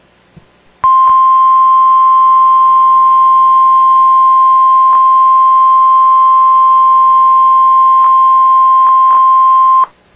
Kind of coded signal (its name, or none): Emergency Alert System (SAME) tones